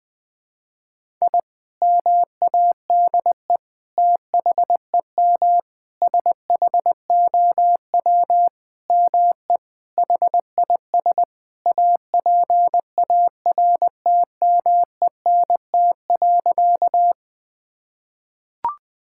Morse code, heard as a keyed tone, about 20 wpm, 700 Hz. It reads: I MADE THEM SHOW ME HIS APARTMENT.
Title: Sentences from Top 1000 Words - Review Part 6 - 20wpm